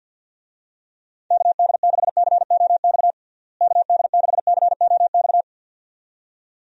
Morse code, heard as a keyed tone, about 51 wpm, 700 Hz. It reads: KD6COX KD6COX
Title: USA Call Signs - 6 Characters - Repeated 2x - 50wpm